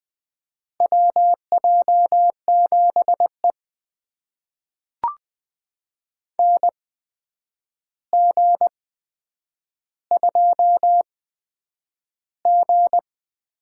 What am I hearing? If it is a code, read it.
Morse code, heard as a keyed tone, about 20 wpm, spaced as for 9 wpm, 700 Hz. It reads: WJ7E NG2G
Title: USA Call Signs - 4 Characters - 8x Character Spacing - 20wpm